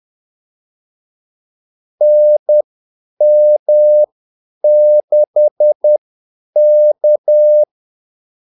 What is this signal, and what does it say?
Morse code, heard as a keyed tone, about 10 wpm, spaced as for 8 wpm, 600 Hz. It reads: NM6K
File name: Callsigns-10-8wpm